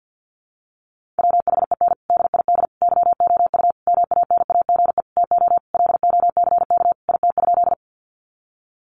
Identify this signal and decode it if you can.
Morse code, heard as a keyed tone, about 50 wpm, 700 Hz. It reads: WHEN DID YOU MANAGE TO LOCK IT?
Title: Sentences from Top 1400 Words 50wpm